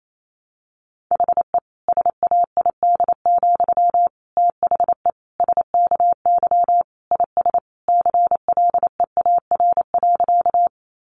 Morse code, heard as a keyed tone, about 28 wpm, 700 Hz. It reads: HE SAID, THE SKY IS CLEAR.
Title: Sentences from Top 1200 Words 28wpm